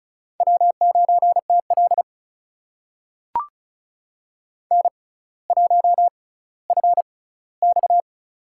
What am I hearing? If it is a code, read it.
Morse code, heard as a keyed tone, about 36 wpm, spaced as for 18 wpm, 700 Hz. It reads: W9TL N1FX
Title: USA Call Signs - 4 Characters - 6x Character Spacing - 35wpm